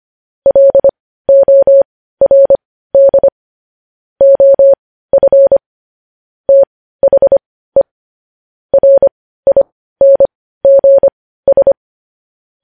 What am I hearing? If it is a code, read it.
Morse code, heard as a keyed tone, about 25 wpm, spaced as for 15 wpm, 550 Hz. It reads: LORD OF THE RINGS
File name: podcast / News headlines-1x-25x15